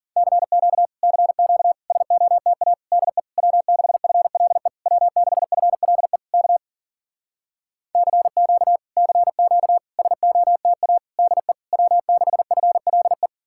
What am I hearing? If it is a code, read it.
Morse code, heard as a keyed tone, about 52 wpm, 700 Hz. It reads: CQ CQ SOTA DE W6FLE W6FLE K CQ CQ SOTA DE W6FLE